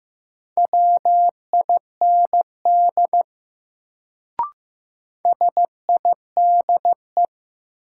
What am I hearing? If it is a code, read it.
Morse code, heard as a keyed tone, about 15 wpm, 700 Hz. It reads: WIND SIDE
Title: Sentences from Top 400 Words - New Word Drill - 15wpm